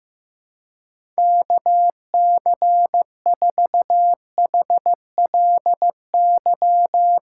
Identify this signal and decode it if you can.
Morse code, heard as a keyed tone, about 15 wpm, 700 Hz. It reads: KC4HLY
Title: Speed Racing - USA 2x3 Call Signs 15wpm